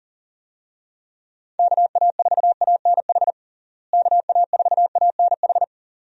Morse code, heard as a keyed tone, about 40 wpm, 700 Hz. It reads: KA4ANH KA4ANH
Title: USA Call Signs - 6 Characters - Repeated 2x - 40wpm